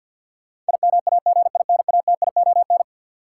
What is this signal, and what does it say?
Morse code, heard as a keyed tone, about 53 wpm, 700 Hz. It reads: IMAGINATION
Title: Sentences from Top 1300 Words - New Word Drill Alt - 50wpm